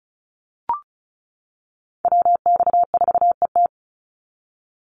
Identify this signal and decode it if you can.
Morse code, heard as a keyed tone, about 34 wpm, 700 Hz. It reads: WX4ET
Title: NA SOTA Chaser Callsigns 35wpm